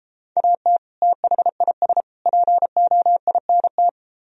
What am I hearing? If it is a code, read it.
Morse code, heard as a keyed tone, about 34 wpm, 700 Hz. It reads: AT THIS POINT